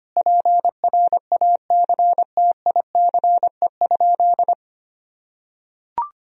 Morse code, heard as a keyed tone, about 25 wpm, 700 Hz. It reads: PRACTICE?
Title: Sentences from Top 1000 Words - Review Part 4 - 25wpm